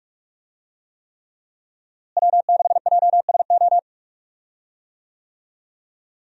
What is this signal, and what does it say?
Morse code, heard as a keyed tone, about 46 wpm, 700 Hz. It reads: W6JSO
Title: USA Call Signs - 5 Characters - Repeated 3x - 45wpm